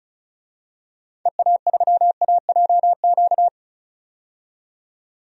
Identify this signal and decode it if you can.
Morse code, heard as a keyed tone, about 34 wpm, 700 Hz. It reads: EA3AJQ
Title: DXCC-100 Call Signs - 6 Characters - 35wpm